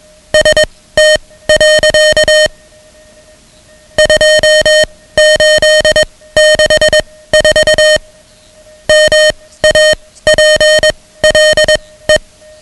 Morse code, heard as a keyed tone, about 21 wpm, spaced as for 16 wpm, 620 Hz, with noise. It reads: ST. 2864 MAPLE
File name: ham radio / CodeQuick / Code Quick 10